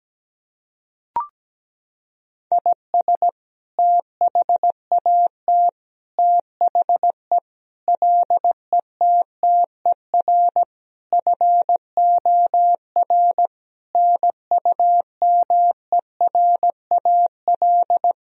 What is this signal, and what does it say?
Morse code, heard as a keyed tone, about 17 wpm, 700 Hz. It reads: IS THAT THE LETTER FOR NUMERAL
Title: Sentences from Top 1000 Words - Review Part 2 - 17wpm